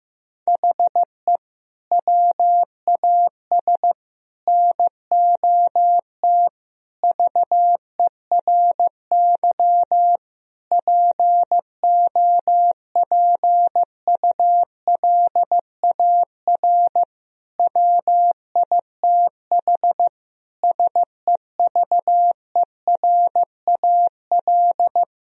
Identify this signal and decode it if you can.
Morse code, heard as a keyed tone, about 15 wpm, 700 Hz. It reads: HE WAS NOT VERY POPULAR WITH SEVERAL
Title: Sentences from Top 1000 Words - Review Part 3 - 15wpm